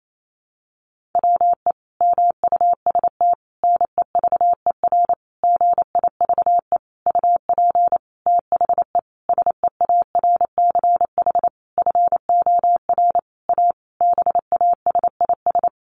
Morse code, heard as a keyed tone, about 28 wpm, 700 Hz. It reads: WE MUST NEVER GIVE UP THE SEARCH FOR A BASIS
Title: Sentences from Top 1500 Words 28wpm